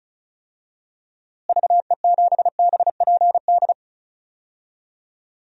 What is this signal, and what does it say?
Morse code, heard as a keyed tone, about 36 wpm, 700 Hz. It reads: VE7BPD
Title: DXCC-100 Call Signs - 6 Characters - Repeated 3x - 35wpm